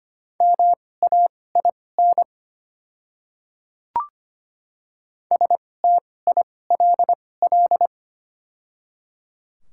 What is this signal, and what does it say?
Morse code, heard as a keyed tone, about 25 wpm, spaced as for 18 wpm, 700 Hz. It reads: MAIN STILL
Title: 2x Character Spacing - Top 300 Words - 25wpm